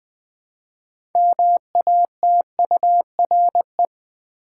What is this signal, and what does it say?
Morse code, heard as a keyed tone, about 20 wpm, 700 Hz. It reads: MATURE
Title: 6 Letter Words - First 3 Letters Given - 20wpm